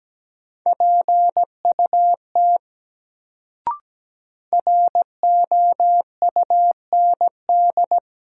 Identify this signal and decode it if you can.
Morse code, heard as a keyed tone, about 17 wpm, 700 Hz. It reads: PUT ROUND
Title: Sentences from Top 200 Words - New Word Drill - 17wpm